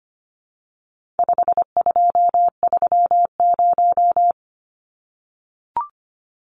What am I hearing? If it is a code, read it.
Morse code, heard as a keyed tone, about 25 wpm, 700 Hz. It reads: 5230
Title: Numbers - 4-Digit 25wpm